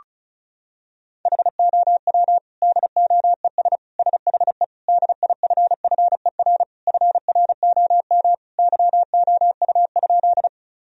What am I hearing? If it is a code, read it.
Morse code, heard as a keyed tone, about 34 wpm, 700 Hz. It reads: HOW DOES SHE DIFFER FROM YOU?
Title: Sentences from Top 1000 Words - Review Part 1 - 35wpm